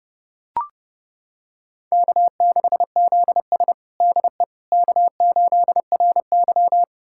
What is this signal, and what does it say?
Morse code, heard as a keyed tone, about 30 wpm, 700 Hz. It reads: K6ZS DE K8RY
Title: USA Callsign DE Callsign - 4 Characters - 30wpm